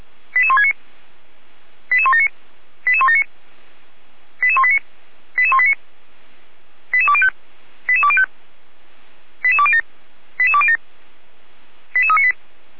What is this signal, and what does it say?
Five-tone selective call (ZVEI), heard as quick runs of five tones, tones of about 70 ms. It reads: 88169 88179 88179 88189 88189 88205 88205 88207 88207 88398